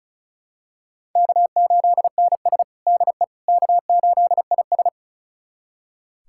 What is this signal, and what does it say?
Morse code, heard as a keyed tone, about 36 wpm, 700 Hz. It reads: K8NS DE K8IS
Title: USA Callsign DE Callsign - 4 Characters - 35wpm